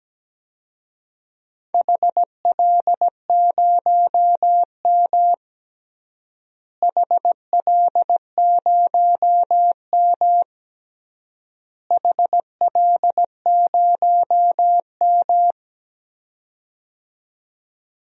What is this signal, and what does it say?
Morse code, heard as a keyed tone, about 17 wpm, 700 Hz. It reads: HL0M HL0M HL0M
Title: DXCC-100 Call Signs - 4 Characters - Repeated 3x - 17wpm